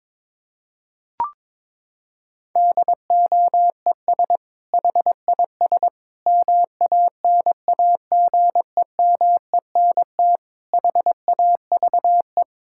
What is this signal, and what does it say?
Morse code, heard as a keyed tone, about 22 wpm, 700 Hz. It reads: DOES HIS MANAGEMENT HAVE
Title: Sentences from Top 1000 Words - Review Part 6 - 22wpm